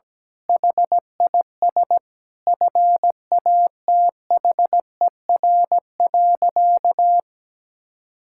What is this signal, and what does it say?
Morse code, heard as a keyed tone, about 17 wpm, 700 Hz. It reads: HIS FATHER.
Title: Sentences from Top 1000 Words - Review Part 5 - 17wpm